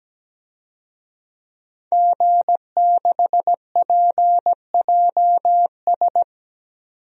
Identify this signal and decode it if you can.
Morse code, heard as a keyed tone, about 17 wpm, 700 Hz. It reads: G6PJS
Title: DXCC-100 Call Signs - 5 Characters - Repeated 2x - 17wpm